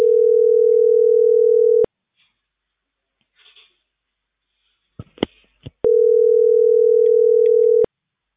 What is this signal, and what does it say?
Ringback tone (North America), 440 + 480 Hz.